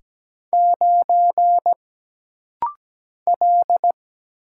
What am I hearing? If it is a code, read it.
Morse code, heard as a keyed tone, about 17 wpm, 700 Hz. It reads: 9 L
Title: Single Letter-Number - Rapid-Fire 17wpm